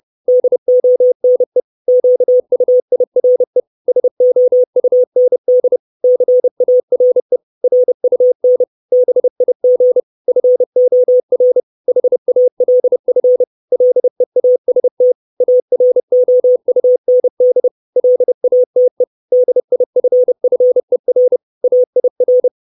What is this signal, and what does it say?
Morse code, heard as a keyed tone, about 30 wpm, 500 Hz. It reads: DONE QUIRE SOUND CARE RUN BIG FOR HALF LEAST AROUND LATE DIFFER AIR
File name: 632 Words 30WPM